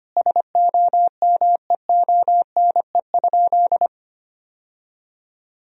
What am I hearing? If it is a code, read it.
Morse code, heard as a keyed tone, about 25 wpm, 700 Hz. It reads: SOMEONE?
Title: Sentences from Top 1000 Words - Review Part 4 - 25wpm